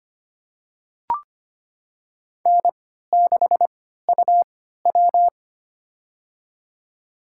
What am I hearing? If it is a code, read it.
Morse code, heard as a keyed tone, about 25 wpm, spaced as for 14 wpm, 700 Hz. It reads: N6UW
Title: USA Call Signs - 3x Character Spacing - 25wpm